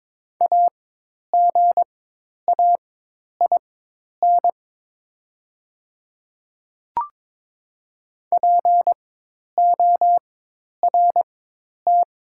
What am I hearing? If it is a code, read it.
Morse code, heard as a keyed tone, about 22 wpm, spaced as for 10 wpm, 700 Hz. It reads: AGAIN PORT
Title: 4x Character Spacing - Top 300 Words - 22wpm